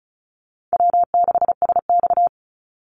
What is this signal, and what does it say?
Morse code, heard as a keyed tone, about 34 wpm, 700 Hz. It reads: W6SX
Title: Easy Call Signs Repeated 3x - 35wpm